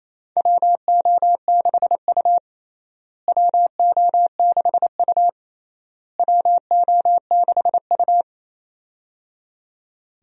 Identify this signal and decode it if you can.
Morse code, heard as a keyed tone, about 28 wpm, 700 Hz. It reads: WO6U WO6U WO6U
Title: USA Call Signs - 4 Characters - Repeated 3x - 28wpm